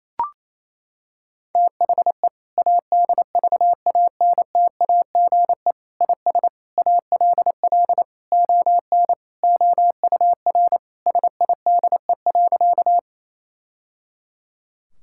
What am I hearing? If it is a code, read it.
Morse code, heard as a keyed tone, about 28 wpm, 700 Hz. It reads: THE ADVANTAGE IS ALL ON OUR SIDE.
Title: Sentences from Top 1000 Words - Review Part 6 - 28wpm